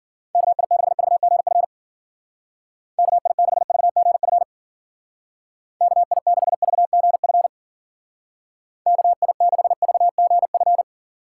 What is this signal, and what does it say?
Morse code, heard as a keyed tone, about 53 wpm, 700 Hz. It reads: KI6VGF KI6VGF KI6VGF KI6VGF